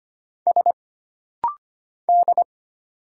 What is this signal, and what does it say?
Morse code, heard as a keyed tone, about 25 wpm, 700 Hz. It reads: S D
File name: Single Character (Letter:Number:Punctuation:Prosign) - Rapid-Fire - 25wpm